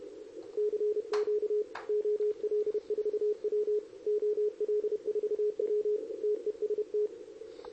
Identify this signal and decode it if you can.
Morse code, heard as a keyed tone, about 31 wpm, 420 Hz. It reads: CQ OL4W OL4W TEST
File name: CQWWCW2023 / RUN